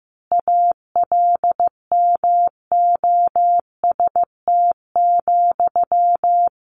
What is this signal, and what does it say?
Morse code, heard as a keyed tone, about 15 wpm, 700 Hz. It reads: ALMOST,
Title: A Sound Of Thunder 15wpm